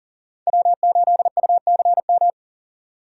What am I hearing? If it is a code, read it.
Morse code, heard as a keyed tone, about 40 wpm, 700 Hz. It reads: W8UCM
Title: USA Call Signs - 6x Character Spacing - 40wpm